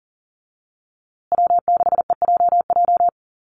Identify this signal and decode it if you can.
Morse code, heard as a keyed tone, about 40 wpm, 700 Hz. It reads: W6EJJ
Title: USA Call Signs Repeated 3x 40wpm